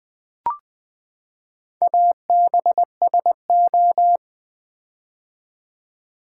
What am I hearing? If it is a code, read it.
Morse code, heard as a keyed tone, about 20 wpm, 700 Hz. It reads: ABSO
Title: Build A Word - 4 Letters At A Time - 8-Letter Words - 20wpm